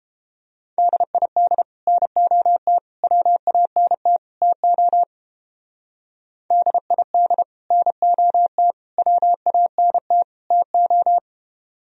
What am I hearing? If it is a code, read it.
Morse code, heard as a keyed tone, about 29 wpm, 700 Hz. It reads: DID NOT WANT TO DID NOT WANT TO